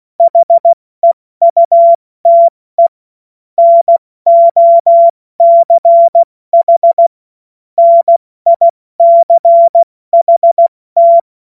Morse code, heard as a keyed tone, about 16 wpm, 680 Hz. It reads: HEUTE NOCH NICHT
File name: Morsekurs / Witze / Witz 20. Januar 2026 70 F80